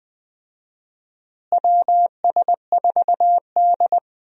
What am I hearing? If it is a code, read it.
Morse code, heard as a keyed tone, about 21 wpm, 700 Hz. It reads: WS4D
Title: USA Call Signs - 4 Characters - Repeated 2x - 20wpm